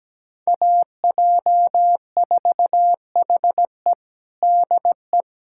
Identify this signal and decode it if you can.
Morse code, heard as a keyed tone, about 17 wpm, 700 Hz. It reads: AJ4HE DE
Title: USA Callsign DE Callsign - 5 Characters - 17wpm